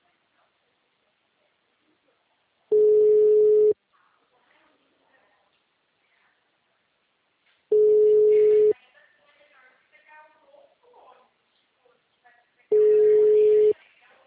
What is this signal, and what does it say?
Ringback tone (Europe, 425 Hz), with speech.